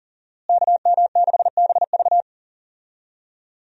Morse code, heard as a keyed tone, about 40 wpm, 700 Hz. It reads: KM6BV
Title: USA Call Signs - 5 Characters - Repeated 2x - 40wpm